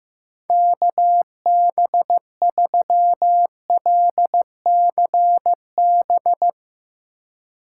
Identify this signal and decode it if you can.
Morse code, heard as a keyed tone, about 15 wpm, 700 Hz. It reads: KB3LCB
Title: USA Call Signs - 6 Characters - 2x Character Spacing - 15wpm